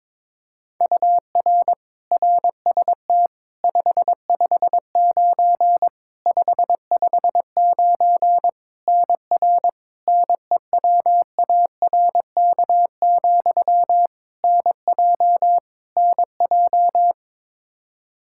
Morse code, heard as a keyed tone, about 22 wpm, 700 Hz. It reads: UR RST 559 559 NR NEWARK, NJ NJ
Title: K1USN SST QSO Practice 22wpm